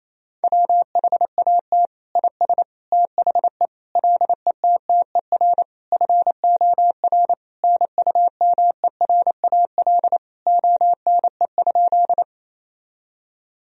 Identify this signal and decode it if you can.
Morse code, heard as a keyed tone, about 28 wpm, 700 Hz. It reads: WHAT IS THE LETTER FOR NUMERAL ONE?
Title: Sentences from Top 1000 Words - Review Part 2 - 28wpm